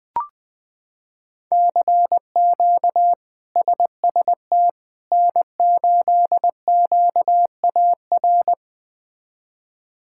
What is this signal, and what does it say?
Morse code, heard as a keyed tone, about 20 wpm, 700 Hz. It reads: CQ SST N8QAR
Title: K1USN SST QSO Practice - 20wpm